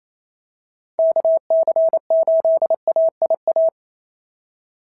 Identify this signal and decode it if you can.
Morse code, heard as a keyed tone, about 25 wpm, 650 Hz. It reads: KC8AIA
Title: SOTA QSO Practice 28wpm